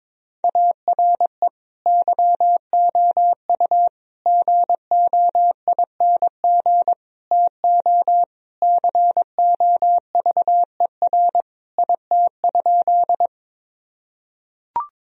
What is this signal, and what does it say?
Morse code, heard as a keyed tone, about 22 wpm, 700 Hz. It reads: ARE YOU GOING TO COVER IT?
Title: Sentences from Top 1000 Words - Review Part 1 - 22wpm